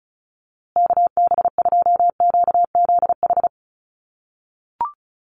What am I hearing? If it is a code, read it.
Morse code, heard as a keyed tone, about 34 wpm, 700 Hz. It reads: KB2QZH
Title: USA Call Signs Repeated 2x 35wpm